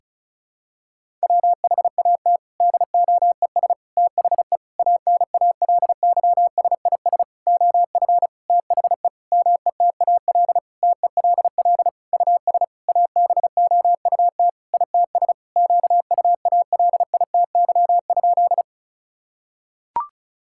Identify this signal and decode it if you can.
Morse code, heard as a keyed tone, about 36 wpm, 700 Hz. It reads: WHAT DOES THE ANALYSIS OF THE METAL TELL US ABOUT ITS QUALITY?
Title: Sentences from Top 1000 Words - Review Part 6 - 35wpm